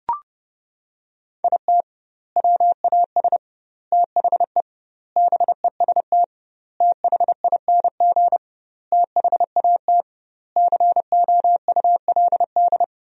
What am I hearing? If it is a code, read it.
Morse code, heard as a keyed tone, about 30 wpm, 700 Hz. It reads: IT WAS THE BEST THING THAT COULD
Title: 2x Word-Spacing - Sentences from Top 400 Words - 30wpm